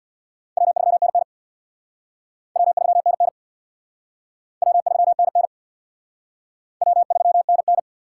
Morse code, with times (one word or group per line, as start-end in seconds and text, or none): 0.6-1.2 W3NN
2.6-3.3 W3NN
4.6-5.5 W3NN
6.8-7.8 W3NN